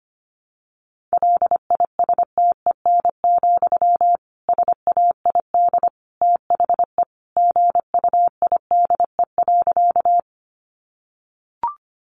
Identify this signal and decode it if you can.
Morse code, heard as a keyed tone, about 25 wpm, 700 Hz. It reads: LISTEN, SAID THE GUIDE.
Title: Sentences from Top 600 Words 25wpm